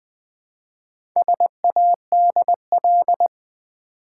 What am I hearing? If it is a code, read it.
Morse code, heard as a keyed tone, about 20 wpm, 700 Hz. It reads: SADL
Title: Build A Word - 1 Letter At A Time - 5-Letter Words - 20wpm